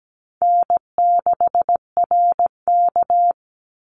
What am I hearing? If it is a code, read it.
Morse code, heard as a keyed tone, about 17 wpm, 700 Hz. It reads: N6RK
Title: Easy Call Signs Repeated 2x - 17wpm